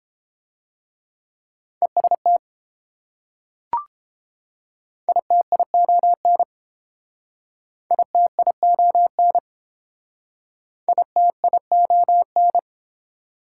Morse code, heard as a keyed tone, about 31 wpm, 690 Hz. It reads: EST ITION ITION ITION